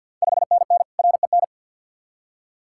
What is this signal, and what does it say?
Morse code, heard as a keyed tone, about 50 wpm, 690 Hz. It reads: 5NN REN
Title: Ontario QSO Party Exchange 50wpm